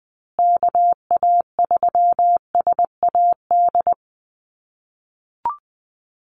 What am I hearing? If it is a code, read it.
Morse code, heard as a keyed tone, about 20 wpm, 700 Hz. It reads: KA3SAD
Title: USA Call Signs Repeated 2x 20wpm